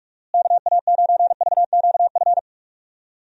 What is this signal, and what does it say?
Morse code, heard as a keyed tone, about 45 wpm, 700 Hz. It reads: KA9VQF
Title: USA Call Signs - 6 Characters - Repeated 2x - 45wpm